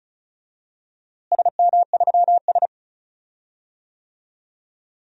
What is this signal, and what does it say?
Morse code, heard as a keyed tone, about 36 wpm, 700 Hz. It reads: SM3S
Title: DXCC-100 Call Signs - 4 Characters - Repeated 3x - 35wpm